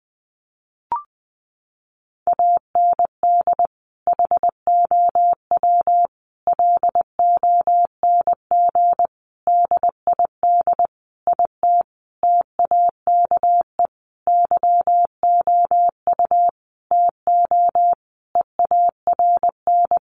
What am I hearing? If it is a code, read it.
Morse code, heard as a keyed tone, about 20 wpm, 700 Hz. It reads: AND HOW LONG DID IT TAKE YOU TO EARN
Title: Sentences from Top 1500 Words 20wpm